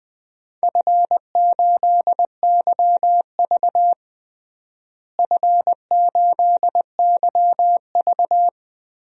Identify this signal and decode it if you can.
Morse code, heard as a keyed tone, about 20 wpm, 700 Hz. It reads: F8YV F8YV
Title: DXCC-100 Call Signs - 4 Characters - Repeated 3x - 20wpm